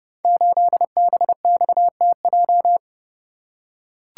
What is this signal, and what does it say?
Morse code, heard as a keyed tone, about 28 wpm, 700 Hz. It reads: 8BXTJ
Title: Code Groups - 5 Letter:Numbers - 30wpm